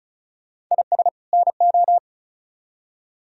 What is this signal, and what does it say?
Morse code, heard as a keyed tone, about 45 wpm, 700 Hz. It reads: IS NO